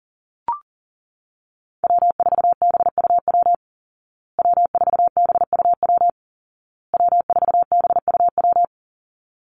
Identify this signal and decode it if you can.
Morse code, heard as a keyed tone, about 40 wpm, 700 Hz. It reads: W4BUW W4BUW W4BUW
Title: Call Signs Repeated 2x - 40wpm